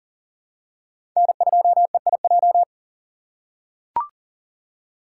Morse code, heard as a keyed tone, about 40 wpm, 700 Hz. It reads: N2EIJ
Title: USA Call Signs - 5 Characters - 4x Character Spacing - 40wpm